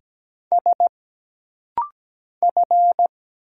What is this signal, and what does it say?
Morse code, heard as a keyed tone, about 17 wpm, 700 Hz. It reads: S F
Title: Single Letter-Number - Rapid-Fire 17wpm